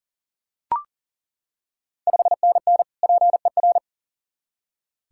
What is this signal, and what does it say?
Morse code, heard as a keyed tone, about 40 wpm, 690 Hz. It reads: 5NN PER
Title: Ontario QSO Party Exchange 40wpm